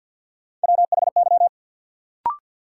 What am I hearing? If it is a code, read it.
Morse code, heard as a keyed tone, about 52 wpm, 700 Hz. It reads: WHY
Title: Sets of 3 Words - New Word Drill - 50wpm